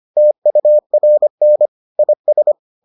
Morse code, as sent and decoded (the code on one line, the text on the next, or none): - ..- .-. -. / .. ...
TURN IS